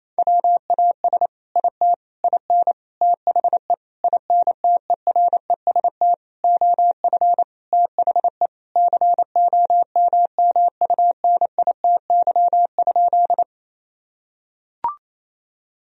Morse code, heard as a keyed tone, about 28 wpm, 700 Hz. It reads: WAS IT IN THE INTEREST OF THE COMMUNITY?
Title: Sentences from Top 1000 Words - Review Part 4 - 28wpm